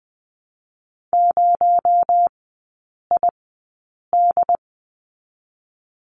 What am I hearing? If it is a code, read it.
Morse code, heard as a keyed tone, about 20 wpm, 700 Hz. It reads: 0 I D
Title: 218 Lesson 35e - Number 0 - Three Characters 20wpm